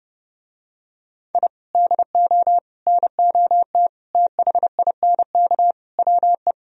Morse code, heard as a keyed tone, about 32 wpm, 700 Hz. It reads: I DO NOT THINK WE